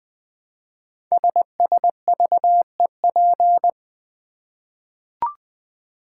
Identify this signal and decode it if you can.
Morse code, heard as a keyed tone, about 20 wpm, 700 Hz. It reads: SSVEP
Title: Code Groups - 5 Letters - 20wpm